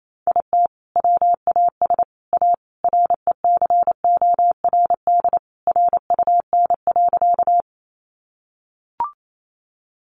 Morse code, heard as a keyed tone, about 28 wpm, 700 Hz. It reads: IT WAS A RECORD RUN.
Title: Sentences from Top 500 Words 28wpm